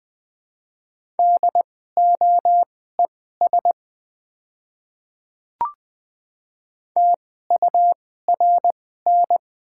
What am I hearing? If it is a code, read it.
Morse code, heard as a keyed tone, about 20 wpm, spaced as for 14 wpm, 700 Hz. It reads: DOES TURN
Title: 2x Character Spacing - Top 300 Words - 20wpm